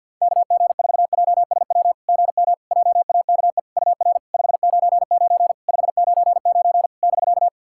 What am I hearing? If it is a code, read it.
Morse code, heard as a keyed tone, about 46 wpm, 690 Hz. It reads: KG4JSW GM JAKE UR 599 599 <BK>
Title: SOTA QSO Practice 50wpm